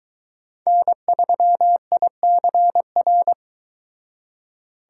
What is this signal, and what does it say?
Morse code, heard as a keyed tone, about 23 wpm, 700 Hz. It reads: N3ICR